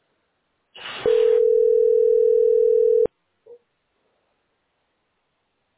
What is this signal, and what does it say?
Ringback tone (North America), 440 + 480 Hz.
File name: 66fd5a69-a2bb-4e02-bfc8